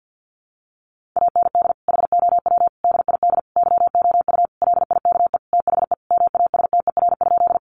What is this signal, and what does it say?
Morse code, heard as a keyed tone, about 50 wpm, 700 Hz. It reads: AND HOW DID YOU LIKE THE MASTER?
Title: Sentences from Top 1300 Words 50wpm